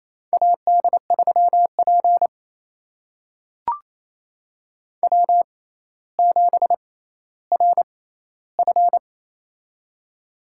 Morse code, heard as a keyed tone, about 28 wpm, spaced as for 14 wpm, 700 Hz. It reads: AD3P W7RF
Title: USA Call Signs - 4 Characters - 6x Character Spacing - 28wpm